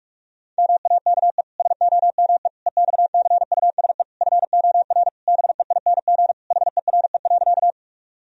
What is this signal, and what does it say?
Morse code, heard as a keyed tone, about 42 wpm, 700 Hz. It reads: MAKE SOME EXCUSE FOR BEING HERE.
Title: Sentences from Top 1000 Words - Review Part 3 - 45wpm